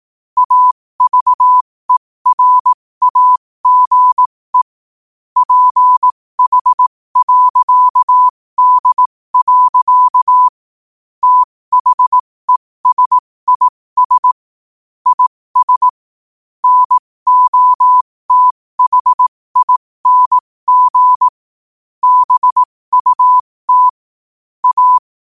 Morse code, heard as a keyed tone, about 18 wpm, spaced as for 15 wpm, 1000 Hz. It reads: AVERAGE PH.D. THESIS IS NOTHING BUT A
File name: qotdfiles / 15wpm